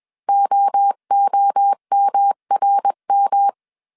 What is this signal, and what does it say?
Morse code, heard as a keyed tone, about 21 wpm, 800 Hz. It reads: OOMRM
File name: 20wpm